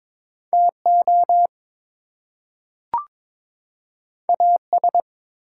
Morse code, heard as a keyed tone, about 22 wpm, 700 Hz. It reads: TO AS
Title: Two Letter Words - 22wpm